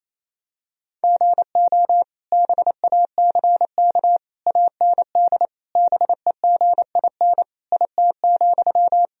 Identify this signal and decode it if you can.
Morse code, heard as a keyed tone, about 28 wpm, 700 Hz. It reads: GO BACK AND BEGIN IT,
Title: Sentences from Top 1000 Words - Review Part 1 - 28wpm